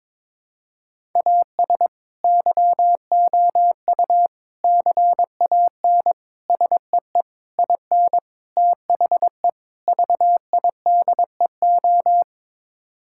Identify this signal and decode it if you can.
Morse code, heard as a keyed tone, about 22 wpm, 700 Hz. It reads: AS YOU CAN SEE IN THE VIDEO
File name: practice / 22wpm / Sets of 7 Words - 22wpm